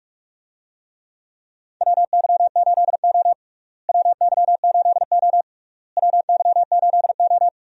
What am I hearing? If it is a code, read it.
Morse code, heard as a keyed tone, about 41 wpm, 700 Hz. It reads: WY8O WY8O WY8O
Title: USA Call Signs - 4 Characters - Repeated 3x - 45wpm